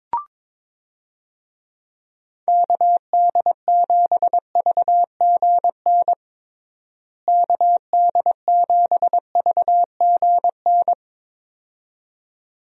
Morse code, heard as a keyed tone, about 22 wpm, 700 Hz. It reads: KD7VGN KD7VGN
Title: USA Call Signs - 6 Characters - Repeated 2x - 22wpm